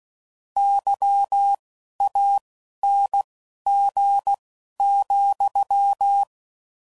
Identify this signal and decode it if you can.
Morse code, heard as a keyed tone, about 16 wpm, spaced as for 12 wpm, 800 Hz. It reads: YANG,
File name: Mixed / Misc / WordPairs